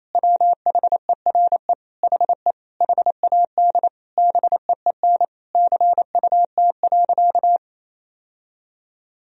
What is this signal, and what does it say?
Morse code, heard as a keyed tone, about 28 wpm, 700 Hz. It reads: WHERE HE HAD BEEN CUT.
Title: Sentences from Top 1000 Words - Review Part 1 - 28wpm